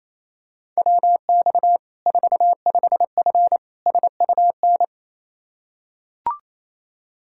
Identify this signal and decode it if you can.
Morse code, heard as a keyed tone, about 28 wpm, 700 Hz. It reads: WX 45F SUN
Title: K1USN SST QSO Practice 28wpm